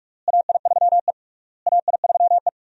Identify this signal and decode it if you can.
Morse code, heard as a keyed tone, about 43 wpm, 700 Hz. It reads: AI3E AI3E
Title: USA Call Signs - 4 Characters - Repeated 3x - 45wpm